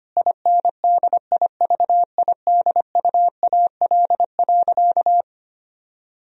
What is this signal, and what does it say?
Morse code, heard as a keyed tone, about 25 wpm, 690 Hz. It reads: INDIVIDUAL.
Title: Sentences from Top 1000 Words - Review Part 4 - 25wpm